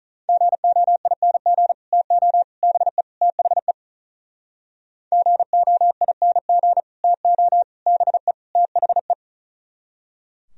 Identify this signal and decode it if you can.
Morse code, heard as a keyed tone, about 46 wpm, 700 Hz. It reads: GOING TO BE THE GOING TO BE THE